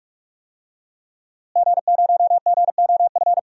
Morse code, heard as a keyed tone, about 42 wpm, 700 Hz. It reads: G0GOF
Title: DXCC-100 Call Signs - 5 Characters - Repeated 2x - 45wpm